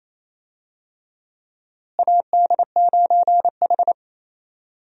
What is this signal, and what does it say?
Morse code, heard as a keyed tone, about 28 wpm, 700 Hz. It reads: AD9H
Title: USA Call Signs - 4 Characters - Repeated 3x - 28wpm